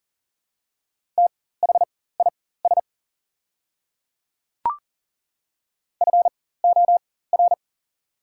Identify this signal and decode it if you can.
Morse code, heard as a keyed tone, about 40 wpm, spaced as for 19 wpm, 700 Hz. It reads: THIS FOR
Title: 4x Character Spacing - Top 200 Words - 40wpm